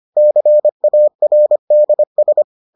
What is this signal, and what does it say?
Morse code, heard as a keyed tone, about 25 wpm, 600 Hz. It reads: CARDS